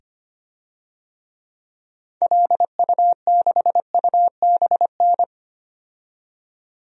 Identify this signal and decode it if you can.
Morse code, heard as a keyed tone, about 25 wpm, 700 Hz. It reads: LU6UBN
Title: DXCC-100 Call Signs - 6 Characters - 25wpm